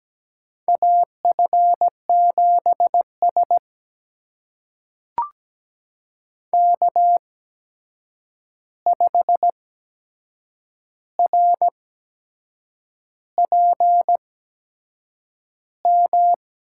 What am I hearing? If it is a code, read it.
Morse code, heard as a keyed tone, about 17 wpm, spaced as for 7 wpm, 700 Hz. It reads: AF7S K5RPM
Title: USA Call Signs - 8x Character Spacing - 17wpm